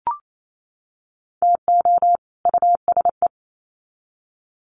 Morse code, heard as a keyed tone, about 28 wpm, 700 Hz. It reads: TO USE
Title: Sets of 2 Words 28wpm